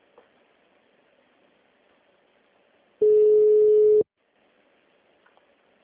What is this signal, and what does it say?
Ringback tone (Europe, 425 Hz).
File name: out- -2008- - - .64